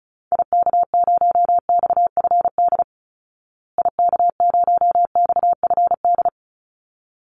Morse code, heard as a keyed tone, about 34 wpm, 700 Hz. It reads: IK0XFD IK0XFD
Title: Call Signs Repeated 3x - 35wpm